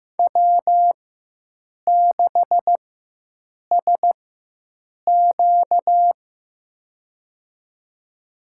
Morse code, heard as a keyed tone, about 15 wpm, spaced as for 7 wpm, 700 Hz. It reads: W6SQ
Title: USA Call Signs - 4 Characters - 4x Character Spacing - 15wpm